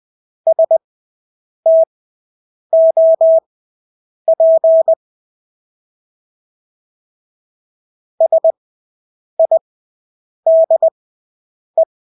Morse code, heard as a keyed tone, about 20 wpm, spaced as for 8 wpm, 650 Hz. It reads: STOP SIDE
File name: Engelske ord 20/8 wpm